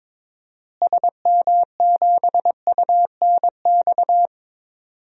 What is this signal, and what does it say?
Morse code, heard as a keyed tone, about 22 wpm, 700 Hz. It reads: SM7UNX
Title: DXCC-100 Call Signs - 6 Characters - Repeated 3x - 22wpm